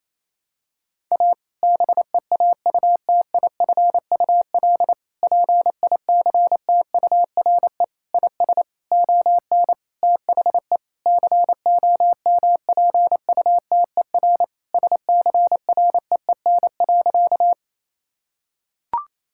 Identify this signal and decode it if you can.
Morse code, heard as a keyed tone, about 28 wpm, 700 Hz. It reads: A BEAUTIFUL PICTURE IS ON THE COMPUTER SCREEN.
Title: Sentences from Top 1000 Words - Review Part 6 - 28wpm